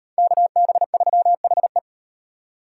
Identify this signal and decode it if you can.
Morse code, heard as a keyed tone, about 38 wpm, 700 Hz. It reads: KB3HE